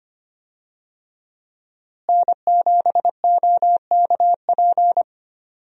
Morse code, heard as a keyed tone, about 25 wpm, 700 Hz. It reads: N7OKP
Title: USA Call Signs - 5 Characters - Repeated 2x - 25wpm